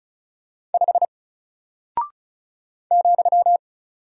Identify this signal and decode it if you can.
Morse code, heard as a keyed tone, about 36 wpm, 700 Hz. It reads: 5 ,
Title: Single Character (Letter:Number:Punctuation:Prosign) Alt 35wpm